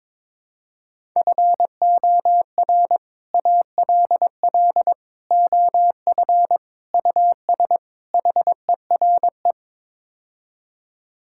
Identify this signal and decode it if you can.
Morse code, heard as a keyed tone, about 22 wpm, 700 Hz. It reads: FOR ALL OF US HERE
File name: practice / 22wpm / Build A Phrase - 5-Word Phrases - 22wpm